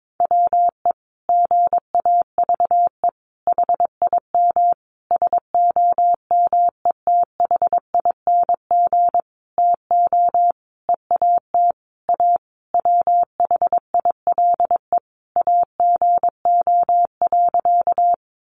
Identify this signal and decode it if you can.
Morse code, heard as a keyed tone, about 22 wpm, 700 Hz. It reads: WE GAVE HIM SOMETHING TO EAT A WHILE AGO.
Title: Sentences from Top 600 Words 22wpm